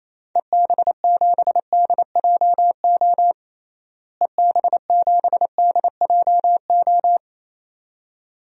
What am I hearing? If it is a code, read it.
Morse code, heard as a keyed tone, about 28 wpm, 700 Hz. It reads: EB7DJO EB7DJO
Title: DXCC-100 Call Signs - 6 Characters - Repeated 2x - 28wpm